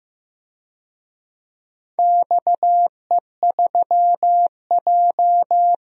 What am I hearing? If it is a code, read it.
Morse code, heard as a keyed tone, about 16 wpm, 700 Hz. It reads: XE3J
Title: DXCC-100 Call Signs - 4 Characters - Repeated 2x - 15wpm